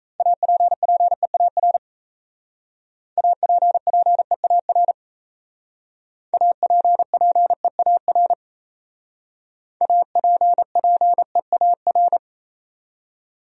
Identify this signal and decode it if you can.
Morse code, heard as a keyed tone, about 35 wpm, 700 Hz. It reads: APPEAR APPEAR APPEAR APPEAR